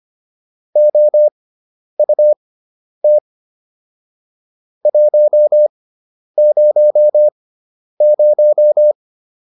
Morse code, heard as a keyed tone, about 25 wpm, spaced as for 10 wpm, 600 Hz. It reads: OUT 100